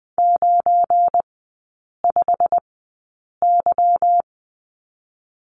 Morse code, heard as a keyed tone, about 20 wpm, spaced as for 8 wpm, 700 Hz. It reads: 95Y
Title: 184 Lesson 30e - Letter b - Three Characters 20wpm